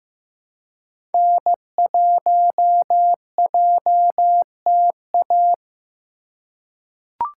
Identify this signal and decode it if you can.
Morse code, heard as a keyed tone, about 15 wpm, 700 Hz. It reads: N1JTA
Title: USA Call Signs - 5 Characters - 2x Character Spacing - 15wpm